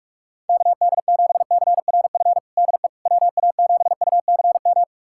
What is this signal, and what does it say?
Morse code, heard as a keyed tone, about 42 wpm, 700 Hz. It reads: KD7CRF DE WA7UCM
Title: USA Callsign DE Callsign - 6 Characters - 45wpm